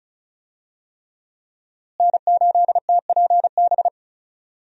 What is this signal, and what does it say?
Morse code, heard as a keyed tone, about 34 wpm, 700 Hz. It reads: N8TPB
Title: USA Call Signs - 5 Characters - Repeated 3x - 35wpm